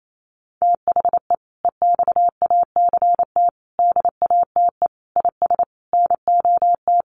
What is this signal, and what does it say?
Morse code, heard as a keyed tone, about 28 wpm, 700 Hz. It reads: THE EXACT DATE IS NOT
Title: Sentences from Top 1300 Words 28wpm